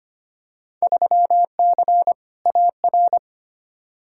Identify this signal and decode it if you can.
Morse code, heard as a keyed tone, about 25 wpm, 700 Hz. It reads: 3C AR
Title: Field Day Exchanges Encore - 25wpm